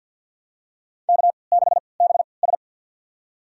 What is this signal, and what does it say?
Morse code, heard as a keyed tone, about 53 wpm, spaced as for 29 wpm, 700 Hz. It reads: K6BS
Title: USA Call Signs - 4 Characters - 3x Character Spacing - 50wpm